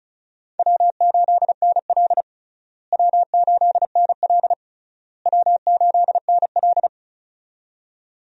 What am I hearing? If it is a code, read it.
Morse code, heard as a keyed tone, about 34 wpm, 700 Hz. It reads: W8NL W8NL W8NL
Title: USA Call Signs - 4 Characters - Repeated 3x - 35wpm